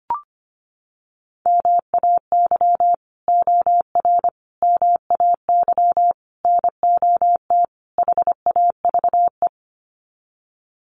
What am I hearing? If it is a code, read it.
Morse code, heard as a keyed tone, about 25 wpm, 700 Hz. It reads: MAY OR MAY NOT HAVE
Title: Sets of 5 Words Encore2 25wpm